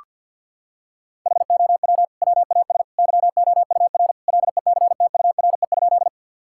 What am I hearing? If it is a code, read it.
Morse code, heard as a keyed tone, about 50 wpm, 700 Hz. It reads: HOW WAS YOUR LECTURE?
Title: Sentences from Top 1600 Words 50wpm